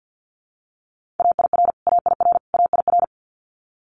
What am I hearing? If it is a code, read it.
Morse code, heard as a keyed tone, about 50 wpm, 700 Hz. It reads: AIR AIR AIR
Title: Multi-letter Phonemes Repeated 3x 50wpm